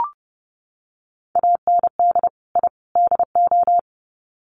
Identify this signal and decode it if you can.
Morse code, heard as a keyed tone, about 30 wpm, 700 Hz. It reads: AND I DO
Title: Sets of 3 Words 30wpm